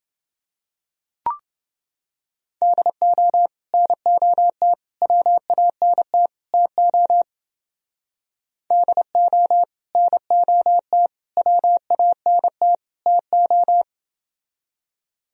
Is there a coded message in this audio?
Yes: Morse code from 2.6 to 13.8 s, heard as a keyed tone, about 29 wpm, 700 Hz. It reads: DO NOT WANT TO DO NOT WANT TO